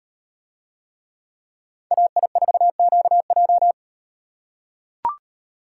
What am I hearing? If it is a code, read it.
Morse code, heard as a keyed tone, about 38 wpm, 700 Hz. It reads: AI4QJ